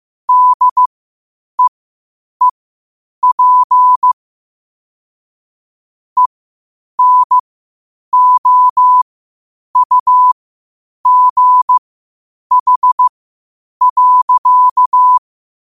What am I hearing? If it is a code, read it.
Morse code, heard as a keyed tone, about 15 wpm, spaced as for 8 wpm, 1000 Hz. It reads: DEEP ENOUGH.
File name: qotdfiles / 8wpm